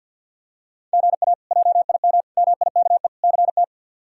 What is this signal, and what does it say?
Morse code, heard as a keyed tone, about 46 wpm, 700 Hz. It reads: GA JIM MIKE CT
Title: K1USN SST QSO Practice - 50wpm